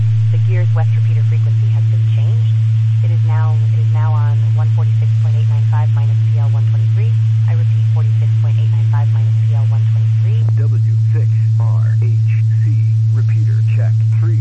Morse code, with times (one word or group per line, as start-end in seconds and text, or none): none